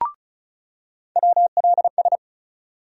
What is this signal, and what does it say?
Morse code, heard as a keyed tone, about 34 wpm, 700 Hz. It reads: WLS
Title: USA Call Sign Postfixes - 3 Letters 35wpm